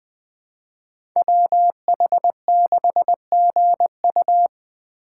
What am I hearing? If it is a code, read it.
Morse code, heard as a keyed tone, about 20 wpm, 700 Hz. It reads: WH6GU
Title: USA Call Signs - 5 Characters - 8x Character Spacing - 20wpm